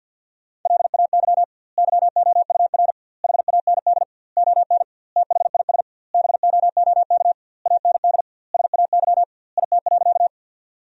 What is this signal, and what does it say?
Morse code, heard as a keyed tone, about 50 wpm, 700 Hz. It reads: LAY YOUR HAND ON THIS BOOK AND SAY IT.
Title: 2x Word-Spacing - Sentences from Top 400 Words - 50wpm